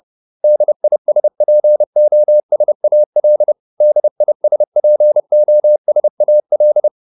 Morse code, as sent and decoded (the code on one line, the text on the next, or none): -.. .. ... .--. --- ... .- .-.. / -.. .. ... .--. --- ... .- .-..
DISPOSAL DISPOSAL